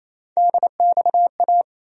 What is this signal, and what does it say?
Morse code, heard as a keyed tone, about 27 wpm, 700 Hz. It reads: DXA
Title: USA Call Sign Postfixes - 3 Letters 28wpm